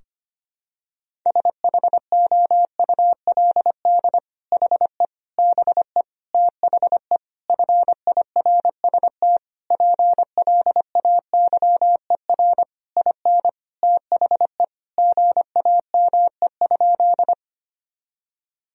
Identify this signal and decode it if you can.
Morse code, heard as a keyed tone, about 25 wpm, 700 Hz. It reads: SHOULD HE BE THE FIRST PLAYER IN THE GAME?
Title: Sentences from Top 1000 Words - Review Part 4 - 25wpm